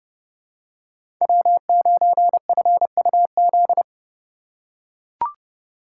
Morse code, heard as a keyed tone, about 30 wpm, 700 Hz. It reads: W9FUZ
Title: USA Call Signs - 5 Characters - 4x Character Spacing - 30wpm